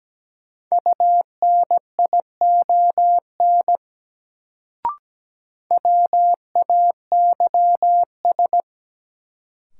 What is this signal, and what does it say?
Morse code, heard as a keyed tone, about 17 wpm, 700 Hz. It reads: UNION WAYS
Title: Sets of 5 Words Encore - New Word Drill - 17wpm